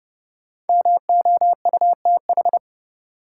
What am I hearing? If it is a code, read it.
Morse code, heard as a keyed tone, about 30 wpm, 700 Hz. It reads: MOUTH
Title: Sentences from Top 1500 Words - New Word Drill Alt - 30wpm